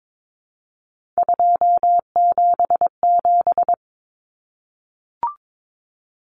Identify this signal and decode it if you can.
Morse code, heard as a keyed tone, about 22 wpm, 700 Hz. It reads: 277
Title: Common Cut Numbers - 3-Digit 22wpm